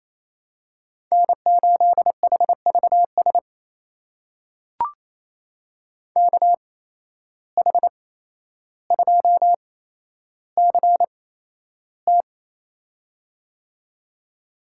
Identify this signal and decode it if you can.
Morse code, heard as a keyed tone, about 28 wpm, spaced as for 12 wpm, 700 Hz. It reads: N8HVS KH2CT
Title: USA Call Signs - 5 Characters - 8x Character Spacing - 28wpm